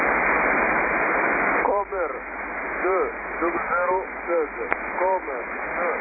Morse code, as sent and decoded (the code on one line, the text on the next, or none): none